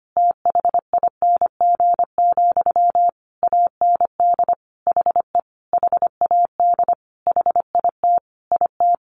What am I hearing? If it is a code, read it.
Morse code, heard as a keyed tone, about 25 wpm, 700 Hz. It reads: THING, AND HE HAD HIT IT
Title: Sentences from Top 1200 Words 25wpm